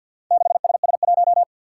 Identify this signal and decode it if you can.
Morse code, heard as a keyed tone, about 48 wpm, 700 Hz. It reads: 6SS1
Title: Four Challenging Letter-Number Combinations 50wpm